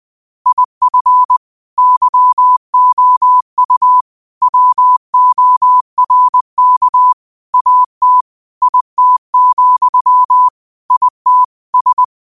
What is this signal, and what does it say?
Morse code, heard as a keyed tone, about 20 wpm, 1000 Hz. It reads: IF YOU WORK AT IT, IT S
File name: qotdfiles / 20wpm